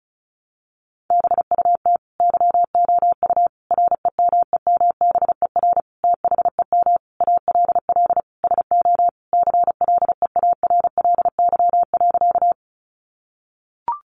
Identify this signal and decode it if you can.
Morse code, heard as a keyed tone, about 34 wpm, 700 Hz. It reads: BUT YOU REMEMBER THEM ALL SO CLEARLY.
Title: Sentences from Top 1300 Words 35wpm